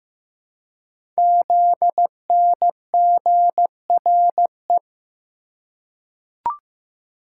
Morse code, heard as a keyed tone, about 15 wpm, 700 Hz. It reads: ZNGRE
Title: Code Groups - 5 Letters - 15wpm